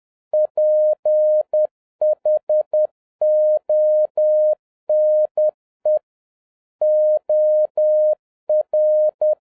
Morse code, heard as a keyed tone, about 10 wpm, 610 Hz. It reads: PHONE OR
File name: 2011-07-mp3 / n21 GREETINGS FROM SINGAPORE 10wpm